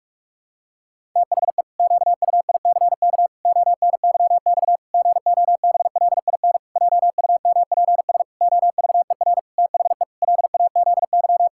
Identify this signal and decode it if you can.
Morse code, heard as a keyed tone, about 41 wpm, 700 Hz. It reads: THE QUICK ONYX GOBLIN JUMPS OVER THE LAZY
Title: Pangram Challenge v2 - 45wpm